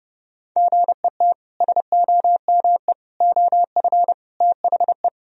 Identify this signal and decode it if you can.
Morse code, heard as a keyed tone, about 30 wpm, 700 Hz. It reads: GET SOME OF THE
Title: Sentences from Top 1000 Words - Review Part 4 - 30wpm